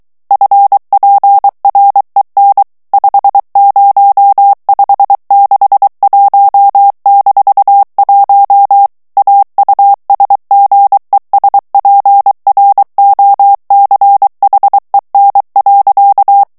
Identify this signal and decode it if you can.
Morse code, heard as a keyed tone, about 23 wpm, 800 Hz. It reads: FPREN 50561-1 AUSGESPROCHEN.
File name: ALLEMAND texte clair 1 vitesse 22 mots minute